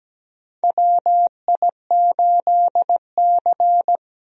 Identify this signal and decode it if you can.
Morse code, heard as a keyed tone, about 17 wpm, 700 Hz. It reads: WI8C
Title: USA Call Signs - 4 Characters - 17wpm